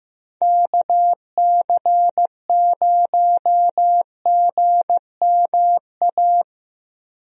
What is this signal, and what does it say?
Morse code, heard as a keyed tone, about 15 wpm, 700 Hz. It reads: KC0GMA
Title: USA Call Signs - 6 Characters - Repeated 2x - 15wpm